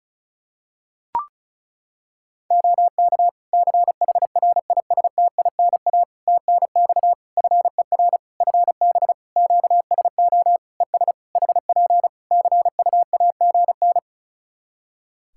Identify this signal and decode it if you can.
Morse code, heard as a keyed tone, about 36 wpm, 700 Hz. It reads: OK CHRISTINA TNX FER FB QSO ES HP CUAGN
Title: Practice QSO - Exchange 4 - Ending - 35wpm